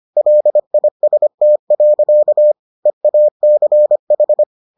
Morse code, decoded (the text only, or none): LIST. EACH